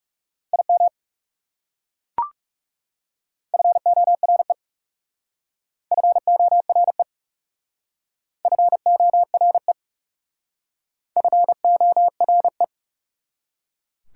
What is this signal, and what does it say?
Morse code, heard as a keyed tone, about 36 wpm, 690 Hz. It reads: IM FORE FORE FORE FORE
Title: Speed-Racing Prefixes and Suffixes 30wpm